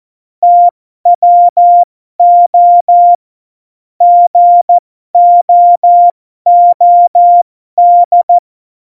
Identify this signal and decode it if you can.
Morse code, heard as a keyed tone, about 14 wpm, spaced as for 12 wpm, 700 Hz. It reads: TWO GOOD